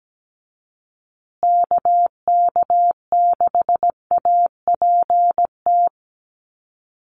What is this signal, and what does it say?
Morse code, heard as a keyed tone, about 17 wpm, 700 Hz. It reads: KK6APT
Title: USA Call Signs Repeated 2x 17wpm